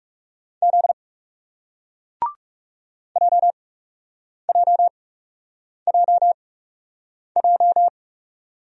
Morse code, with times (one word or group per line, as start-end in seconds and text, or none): 0.6-0.9 Z
3.2-7.9 JJJJ